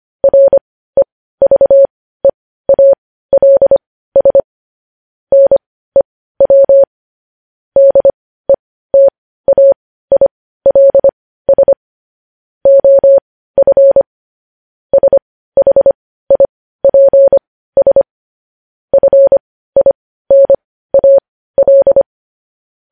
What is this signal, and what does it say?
Morse code, heard as a keyed tone, about 25 wpm, spaced as for 15 wpm, 550 Hz. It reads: REVEALS NEW DETAILS OF SHIPS FINAL
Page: news-headlines-in-morse-code-at-15-wpm